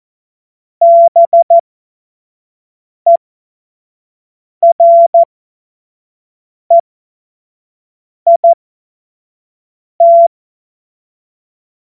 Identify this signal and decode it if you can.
Morse code, heard as a keyed tone, about 14 wpm, spaced as for 5 wpm, 680 Hz. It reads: BEREIT
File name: Morsekurs / Witze / Witz 19. April 2022 25 F70